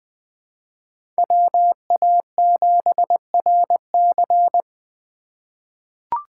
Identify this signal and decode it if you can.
Morse code, heard as a keyed tone, about 20 wpm, 700 Hz. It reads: WA7RC
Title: USA Call Signs - 5 Characters - 3x Character Spacing - 20wpm